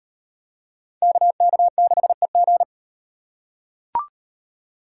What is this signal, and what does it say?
Morse code, heard as a keyed tone, about 37 wpm, 700 Hz. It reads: KK6EG